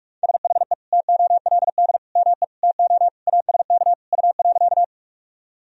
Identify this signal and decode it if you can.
Morse code, heard as a keyed tone, about 46 wpm, 700 Hz. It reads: SHE TOLD ME TO ASK U.
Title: Sentences from Top 400 Words and Abbreviations 45wpm